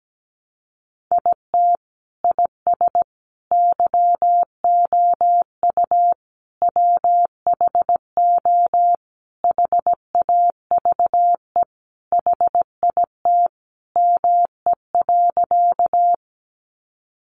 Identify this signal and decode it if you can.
Morse code, heard as a keyed tone, about 17 wpm, 700 Hz. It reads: IT IS YOU WHO HAVE HIT ME.
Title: Sentences from Top 1200 Words 17wpm